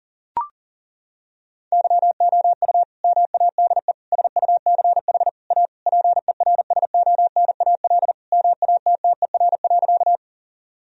Morse code, heard as a keyed tone, about 40 wpm, 700 Hz. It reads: YOU MADE SUCH A PERSONAL MATTER.
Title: Sentences from Top 1000 Words - Review Part 4 - 40wpm